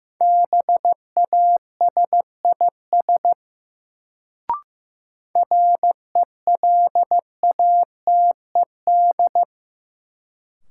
Morse code, heard as a keyed tone, about 15 wpm, 700 Hz. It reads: BASIS RELATED
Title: Sentences from Top 1000 Words - Review Part 5 - New Word Drill - 15wpm